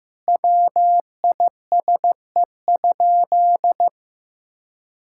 Morse code, heard as a keyed tone, about 15 wpm, 700 Hz. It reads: WISE?
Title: Sentences from Top 1600 Words 15wpm